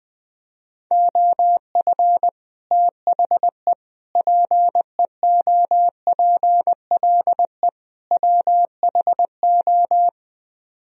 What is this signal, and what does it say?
Morse code, heard as a keyed tone, about 20 wpm, 700 Hz. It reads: OF THE PEOPLE WHO
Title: Speed-Racing - Sets of 4 Words 17wpm